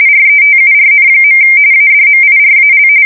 RTTY, about 45.45 baud. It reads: RETURN. ADMIRAL
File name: fsk170 / USS-New-Jersey